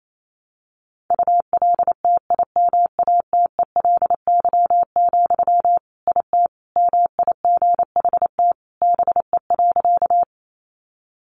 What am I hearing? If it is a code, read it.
Morse code, heard as a keyed tone, about 28 wpm, 700 Hz. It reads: ULTIMATELY, IT MIGHT BE.
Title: Sentences from Top 1200 Words 28wpm